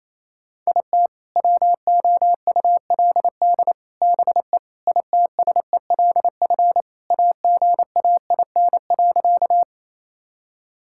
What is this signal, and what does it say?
Morse code, heard as a keyed tone, about 28 wpm, 700 Hz. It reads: IT WOULD BE ITSELF AGAIN.
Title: Sentences from Top 1000 Words - Review Part 5 - 28wpm